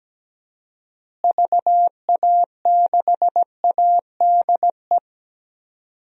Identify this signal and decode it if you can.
Morse code, heard as a keyed tone, about 17 wpm, 700 Hz. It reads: VA6ADE
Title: DXCC-100 Call Signs - 6 Characters - Repeated 2x - 17wpm